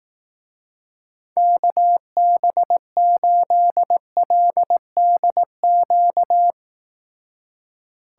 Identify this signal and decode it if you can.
Morse code, heard as a keyed tone, about 18 wpm, 700 Hz. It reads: KB8LDQ
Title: Speed Racing - USA Call Signs 15wpm